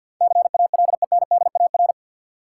Morse code, heard as a keyed tone, about 52 wpm, 700 Hz. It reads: CALENDAR
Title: Sentences from Top 1600 Words - New Word Drill - 50wpm